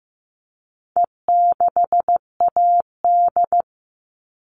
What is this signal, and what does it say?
Morse code, heard as a keyed tone, about 15 wpm, 700 Hz. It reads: E6AD
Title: All Cut Numbers - 4-Digit 15wpm